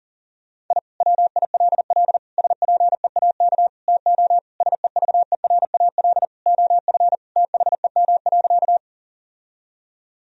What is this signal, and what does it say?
Morse code, heard as a keyed tone, about 40 wpm, 700 Hz. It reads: I WILL SPEAK TO SEVERAL OF THEM.
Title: Sentences from Top 1000 Words - Review Part 2 - 40wpm